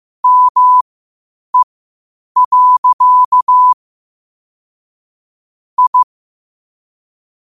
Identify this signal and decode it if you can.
Morse code, heard as a keyed tone, about 15 wpm, spaced as for 8 wpm, 1000 Hz. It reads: ME. I
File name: qotdfiles / 8wpm